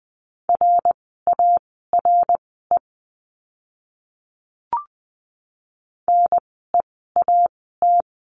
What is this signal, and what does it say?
Morse code, heard as a keyed tone, about 20 wpm, spaced as for 14 wpm, 700 Hz. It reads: RARE NEAT
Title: 054 Lesson 10f - Letter r - Words 20wpm